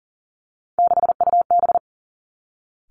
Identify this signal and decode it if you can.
Morse code, heard as a keyed tone, about 39 wpm, 700 Hz. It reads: 6UB
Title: Three Challenging Letter-Number Combinations 40wpm